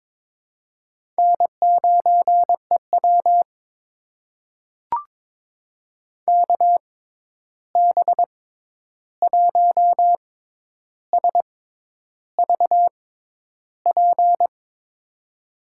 Morse code, heard as a keyed tone, about 22 wpm, spaced as for 10 wpm, 700 Hz. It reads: N9EW KB1SVP
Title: USA Call Signs - 6x Character Spacing - 22wpm